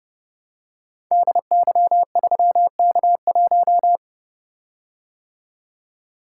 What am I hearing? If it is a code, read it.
Morse code, heard as a keyed tone, about 29 wpm, 700 Hz. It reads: DY3K1
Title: Code Groups - 5 Letter:Numbers - 30wpm